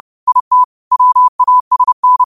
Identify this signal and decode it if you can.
Morse code, heard as a keyed tone, about 30 wpm, 1000 Hz. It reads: IT WASN